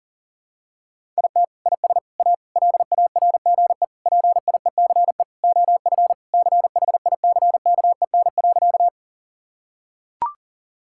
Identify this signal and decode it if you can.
Morse code, heard as a keyed tone, about 40 wpm, 700 Hz. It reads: IT IS A LARGE PIECE OF CHICKEN.
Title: Sentences from Top 1000 Words - Review Part 5 - 40wpm